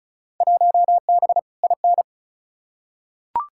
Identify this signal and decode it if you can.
Morse code, heard as a keyed tone, about 34 wpm, 700 Hz. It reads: 1B IN
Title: Field Day Exchanges - 35wpm